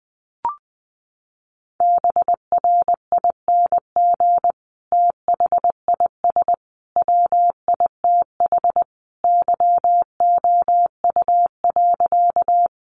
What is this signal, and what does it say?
Morse code, heard as a keyed tone, about 20 wpm, 700 Hz. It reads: BRING THIS WITH YOU.
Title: Sentences from Top 500 Words 20wpm